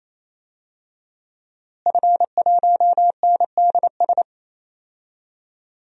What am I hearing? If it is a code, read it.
Morse code, heard as a keyed tone, about 28 wpm, 700 Hz. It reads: F1NDS
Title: DXCC-100 Call Signs - 5 Characters - Repeated 3x - 28wpm